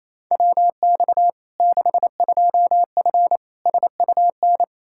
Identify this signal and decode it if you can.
Morse code, heard as a keyed tone, about 28 wpm, 700 Hz. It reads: WX 62F SUN
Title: Practice QSO - Exchange 2 - Info - 28wpm